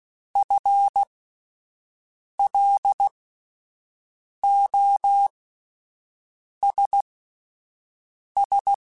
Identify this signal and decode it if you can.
Morse code, heard as a keyed tone, about 16 wpm, spaced as for 5 wpm, 800 Hz. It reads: FLOSS
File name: Spaced / FL Runs sp mp3s / FL Runs sp @10